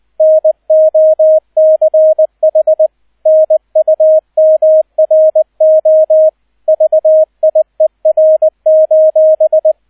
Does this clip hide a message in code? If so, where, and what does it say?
Yes: Morse code from 0.2 to 9.7 s, heard as a keyed tone, about 20 wpm, 630 Hz, with radio noise. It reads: NOCH NUMRO VIER: